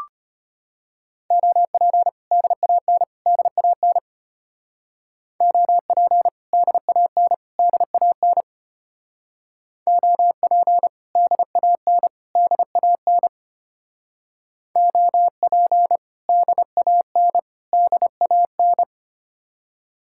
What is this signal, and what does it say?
Morse code, heard as a keyed tone, about 32 wpm, 700 Hz. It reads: OP DAN DAN OP DAN DAN OP DAN DAN OP DAN DAN